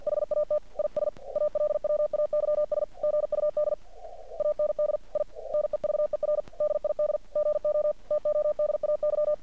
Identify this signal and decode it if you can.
Morse code, heard as a keyed tone, about 50 wpm, 600 Hz. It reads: HAT IS ALWAYS GUD AND I NEVER DID GO TODAY
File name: archived / CRSnet